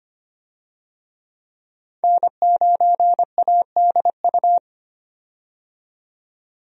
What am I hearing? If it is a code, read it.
Morse code, heard as a keyed tone, about 25 wpm, 700 Hz. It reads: N9ADU
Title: USA Call Signs - 5 Characters - 25wpm